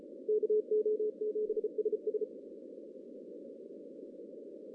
Morse code, heard as a keyed tone, about 34 wpm, 430 Hz. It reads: KO7SS
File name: 2015 cq ww cw / 2015 cqwww k5tr - 0-18